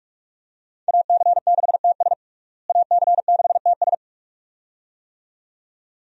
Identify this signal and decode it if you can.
Morse code, heard as a keyed tone, about 41 wpm, 700 Hz. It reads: AC6TS AC6TS
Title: USA Call Signs - 5 Characters - Repeated 2x - 45wpm